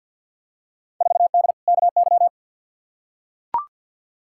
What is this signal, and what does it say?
Morse code, heard as a keyed tone, about 46 wpm, 700 Hz. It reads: 4D KY
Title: Field Day Exchanges Encore - 50wpm